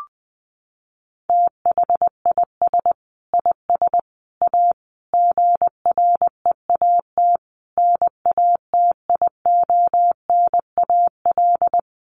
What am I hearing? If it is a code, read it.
Morse code, heard as a keyed tone, about 20 wpm, 700 Hz. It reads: THIS IS A GREAT NATIONAL
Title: Sentences from Top 1300 Words 20wpm